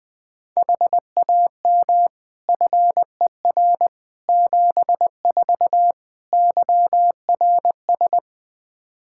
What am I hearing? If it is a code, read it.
Morse code, heard as a keyed tone, about 20 wpm, 700 Hz. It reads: HAM FER 74 YRS
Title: Practice QSO - Exchange 3 - More Info - 20wpm